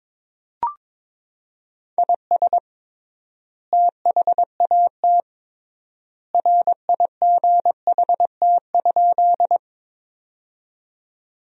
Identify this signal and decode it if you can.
Morse code, heard as a keyed tone, about 21 wpm, 700 Hz. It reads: IS THAT RIGHT?
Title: 3x Word-Spacing - Sentences from Top 200 Words - 22wpm